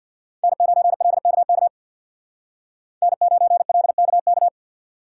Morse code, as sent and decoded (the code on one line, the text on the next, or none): -. ----. .-.. -.- -.- / -. ----. .-.. -.- -.-
N9LKK N9LKK